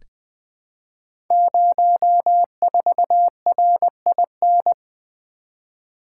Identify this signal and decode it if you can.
Morse code, heard as a keyed tone, about 20 wpm, 700 Hz. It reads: 04RIN
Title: Code Groups - 5 Letter:Numbers - 20wpm